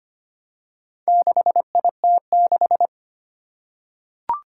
Morse code, heard as a keyed tone, about 25 wpm, 700 Hz. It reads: 6IT6
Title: Four Challenging Letter-Number Combinations 25wpm